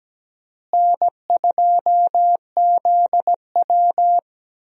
Morse code, heard as a keyed tone, about 17 wpm, 700 Hz. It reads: N2ZW
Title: USA Call Signs - 4 Characters - Repeated 2x - 17wpm